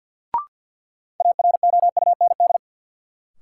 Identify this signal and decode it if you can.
Morse code, heard as a keyed tone, about 52 wpm, 700 Hz. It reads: AROUND
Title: Sentences from Top 600 Words - New Word Drill - 50wpm